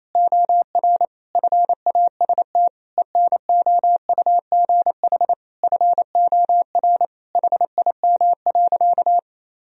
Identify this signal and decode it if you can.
Morse code, heard as a keyed tone, about 28 wpm, 700 Hz. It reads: OR FAST ENOUGH FOR HIM.
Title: Sentences from Top 1000 Words - Review Part 2 - 28wpm